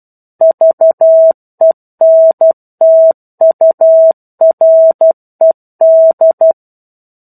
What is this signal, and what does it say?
Morse code, heard as a keyed tone, about 12 wpm, 650 Hz. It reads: VENTURED
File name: 01-Chapter-01 12 WPM